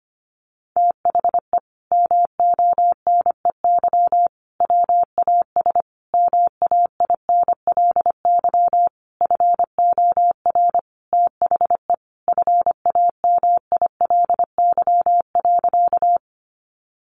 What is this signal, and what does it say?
Morse code, heard as a keyed tone, about 25 wpm, 700 Hz. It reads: THE MONEY WAS MAINLY FOR THE FAMILY.
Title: Sentences from Top 1400 Words 25wpm